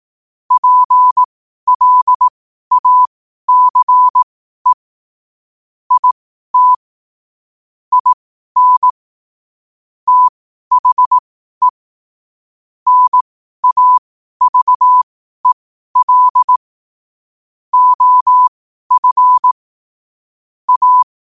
Morse code, heard as a keyed tone, about 18 wpm, spaced as for 12 wpm, 1000 Hz. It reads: PLACE IT IN THE NAVEL OF A
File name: qotdfiles / 12wpm